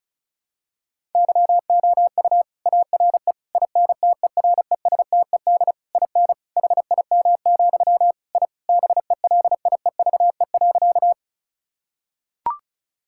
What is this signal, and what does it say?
Morse code, heard as a keyed tone, about 36 wpm, 690 Hz. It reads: YOU ARE INTERESTED IN HIM, I BELIEVE.
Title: Sentences from Top 1000 Words - Review Part 6 - 35wpm